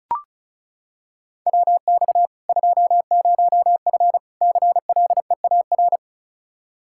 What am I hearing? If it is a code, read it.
Morse code, heard as a keyed tone, about 36 wpm, 700 Hz. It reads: WX 20F CLEAR
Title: K1USN SST QSO Practice 35wpm